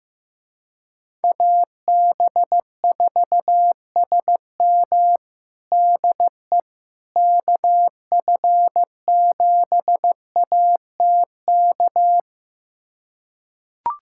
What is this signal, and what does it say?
Morse code, heard as a keyed tone, about 15 wpm, 700 Hz. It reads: AB4SM DE KF7ATK
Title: USA Callsign DE Callsign 15wpm